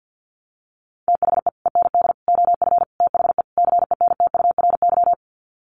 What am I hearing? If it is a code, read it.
Morse code, heard as a keyed tone, about 50 wpm, 700 Hz. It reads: THE END OF THE CENTURY
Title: Sets of 5 Words Encore2 50wpm